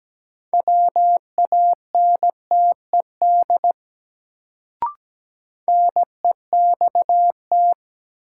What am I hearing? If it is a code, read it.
Morse code, heard as a keyed tone, about 17 wpm, 700 Hz. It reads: WANTED NEXT
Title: Sets of 2 Words - New Word Drill - 17wpm